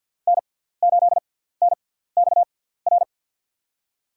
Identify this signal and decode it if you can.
Morse code, heard as a keyed tone, about 47 wpm, spaced as for 17 wpm, 700 Hz. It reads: N8NXR
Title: USA Call Signs - 6x Character Spacing - 50wpm